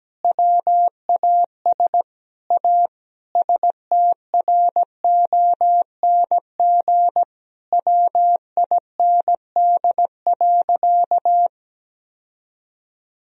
Morse code, heard as a keyed tone, about 17 wpm, 700 Hz. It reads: WAS A STRONG WIND.
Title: Sentences from Top 1000 Words - Review Part 4 - 17wpm